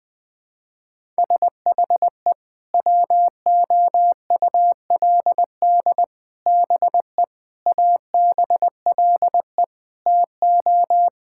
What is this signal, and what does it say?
Morse code, heard as a keyed tone, about 20 wpm, 700 Hz. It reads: SHE WOULD BE ABLE TO
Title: Sentences with 4-Word Phrases - Spoken Beginning - 20wpm